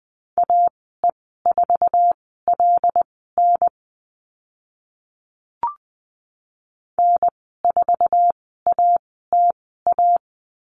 Morse code, heard as a keyed tone, about 20 wpm, spaced as for 15 wpm, 700 Hz. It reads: AE4LN N4ATA
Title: 088 Lesson 15g - Number 5 - Call Signs 20wpm